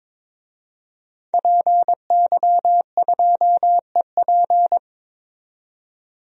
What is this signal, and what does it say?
Morse code, heard as a keyed tone, about 22 wpm, 700 Hz. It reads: PY2EP
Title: DXCC-100 Call Signs - 5 Characters - 22wpm